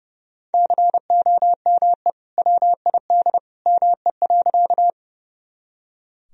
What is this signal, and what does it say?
Morse code, heard as a keyed tone, about 31 wpm, 700 Hz. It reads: COME WID ME.
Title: Sentences from Top 200 Words and Abbreviations 30wpm